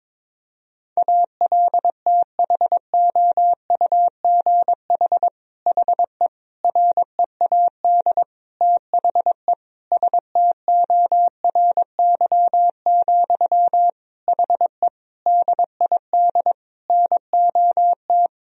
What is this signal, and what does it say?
Morse code, heard as a keyed tone, about 22 wpm, 700 Hz. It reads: ALTHOUGH HE READ THE STORY, HE DID NOT
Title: Sentences from Top 1000 Words - Review Part 3 - 22wpm